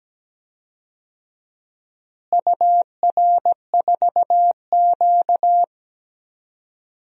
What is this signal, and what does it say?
Morse code, heard as a keyed tone, about 17 wpm, 700 Hz. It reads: UR4Q
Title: DXCC-100 Call Signs - 4 Characters - Repeated 2x - 17wpm